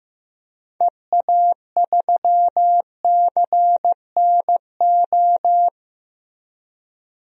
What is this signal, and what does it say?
Morse code, heard as a keyed tone, about 15 wpm, 700 Hz. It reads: EA3CNO
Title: DXCC-100 Call Signs - 6 Characters - Repeated 2x - 15wpm